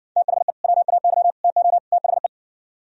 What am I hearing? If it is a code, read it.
Morse code, heard as a keyed tone, about 60 wpm, 700 Hz. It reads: THE WAY TO THE